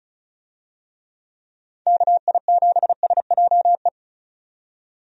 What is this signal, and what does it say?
Morse code, heard as a keyed tone, about 36 wpm, 700 Hz. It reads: KI7SJE
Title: USA Call Signs - 6 Characters - 35wpm